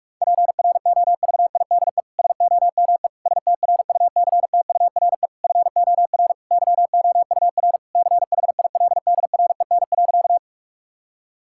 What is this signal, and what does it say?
Morse code, heard as a keyed tone, about 46 wpm, 700 Hz. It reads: PROVIDE SOME STRUCTURE FOR YOUR CHILDREN.
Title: Sentences from Top 1000 Words - Review Part 6 - 45wpm